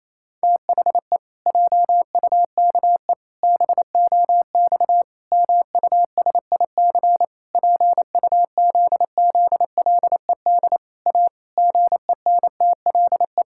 Morse code, heard as a keyed tone, about 28 wpm, 700 Hz. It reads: THE JUKE BOX MUSIC PUZZLED A GENTLE
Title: Pangram Challenge v2 - 28wpm